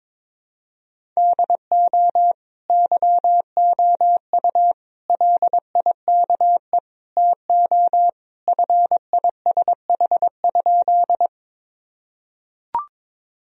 Morse code, heard as a keyed tone, about 22 wpm, 700 Hz. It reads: DO YOU LIKE TO FISH?
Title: Sentences from Top 1000 Words - Review Part 1 - 22wpm